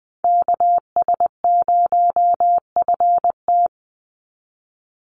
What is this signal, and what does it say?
Morse code, heard as a keyed tone, about 20 wpm, 700 Hz. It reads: KS0FT
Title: USA Call Signs 20wpm